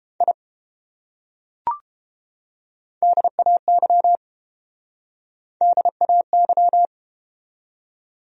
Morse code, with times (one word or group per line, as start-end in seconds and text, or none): 0.2-0.3 I
3.0-4.2 DAY
5.6-6.9 DAY